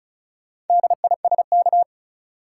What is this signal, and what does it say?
Morse code, heard as a keyed tone, about 36 wpm, 700 Hz. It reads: DISK
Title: Sentences from Top 1000 Words - Review Part 6 - New Word Drill - 35wpm